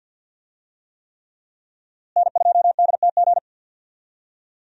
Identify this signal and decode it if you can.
Morse code, heard as a keyed tone, about 49 wpm, 700 Hz. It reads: N2DTG
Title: USA Call Signs - 5 Characters - Repeated 3x - 50wpm